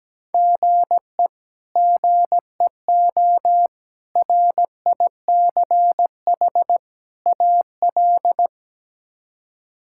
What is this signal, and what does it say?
Morse code, heard as a keyed tone, about 17 wpm, 700 Hz. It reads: GE GEO RICH AL
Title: K1USN SST QSO Practice - 17wpm